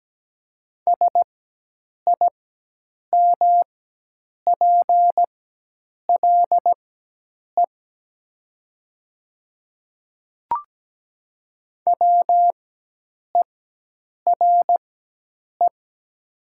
Morse code, heard as a keyed tone, about 17 wpm, spaced as for 8 wpm, 700 Hz. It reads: SIMPLE WERE
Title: 4x Character Spacing - Top 500 Words - 17wpm